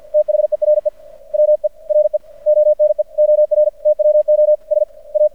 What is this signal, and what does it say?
Morse code, heard as a keyed tone, about 50 wpm, spaced as for 43 wpm, 600 Hz. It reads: THERE WERE ONE OR TWO SA
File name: archived / CRSnet